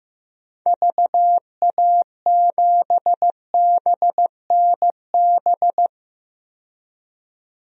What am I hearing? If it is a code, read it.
Morse code, heard as a keyed tone, about 15 wpm, 700 Hz. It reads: VA7BNB
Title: DXCC-100 Call Signs - 6 Characters - 15wpm